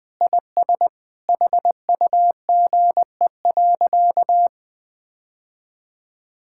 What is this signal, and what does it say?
Morse code, heard as a keyed tone, about 21 wpm, 700 Hz. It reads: IS HUGE.
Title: Sentences with 5-Word Phrases - 20wpm